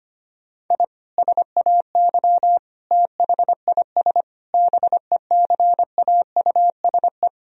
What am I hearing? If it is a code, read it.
Morse code, heard as a keyed tone, about 25 wpm, 700 Hz. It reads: I SAY THIS BECAUSE
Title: Sentences from Top 1000 Words - Review Part 3 - 25wpm